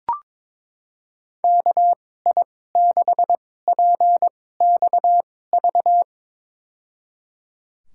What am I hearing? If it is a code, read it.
Morse code, heard as a keyed tone, about 22 wpm, spaced as for 16 wpm, 700 Hz. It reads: KI6PXV
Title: USA Call Signs - 6 Characters - 2x Character Spacing - 22wpm